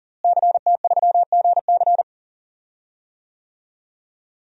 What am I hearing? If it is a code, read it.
Morse code, heard as a keyed tone, about 38 wpm, 700 Hz. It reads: CT3GC
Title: DXCC-100 Call Signs - 5 Characters - Repeated 3x - 40wpm